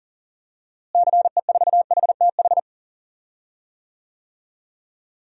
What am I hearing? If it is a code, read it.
Morse code, heard as a keyed tone, about 40 wpm, 700 Hz. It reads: CE4HTH
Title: DXCC-100 Call Signs - 6 Characters - Repeated 2x - 40wpm